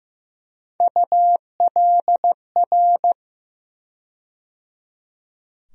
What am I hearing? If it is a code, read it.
Morse code, heard as a keyed tone, about 15 wpm, 700 Hz. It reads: ULR
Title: USA Call Sign Postfixes - 3 Letters 15wpm